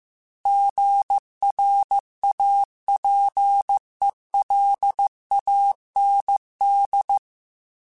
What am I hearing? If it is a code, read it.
Morse code, heard as a keyed tone, about 15 wpm, 800 Hz. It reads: GRAPELAND